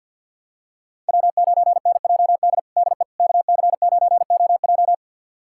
Morse code, heard as a keyed tone, about 48 wpm, 700 Hz. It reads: W9NJD DE KC9OJ
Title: USA Callsign DE Callsign - 5 Characters - 50wpm